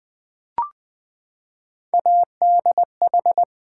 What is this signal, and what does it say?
Morse code, heard as a keyed tone, about 20 wpm, 700 Hz. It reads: ADH
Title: Build A Word - 3 Letters At A Time - 6-Letter Words - 20wpm